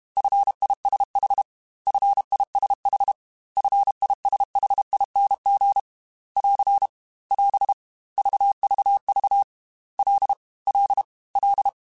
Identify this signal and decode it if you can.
Morse code, heard as a keyed tone, about 32 wpm, 800 Hz. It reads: FISH FISH FISHING <AR> <AS> VVV L L L
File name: F and L / F and L mp3 / F and L Prefixes 3x2 Twice @30